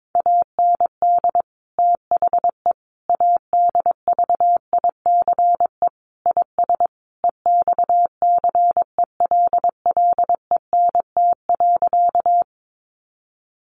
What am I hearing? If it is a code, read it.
Morse code, heard as a keyed tone, about 22 wpm, 700 Hz. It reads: AND THE ADVICE IS EXCELLENT.
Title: Sentences from Top 1100 Words 22wpm